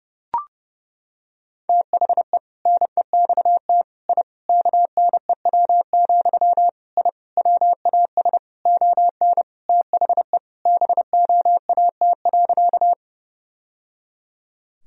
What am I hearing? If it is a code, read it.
Morse code, heard as a keyed tone, about 30 wpm, 700 Hz. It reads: THE NEXT I KNEW, I WAS ON THE BOAT.
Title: Sentences from Top 1000 Words - Review Part 3 - 30wpm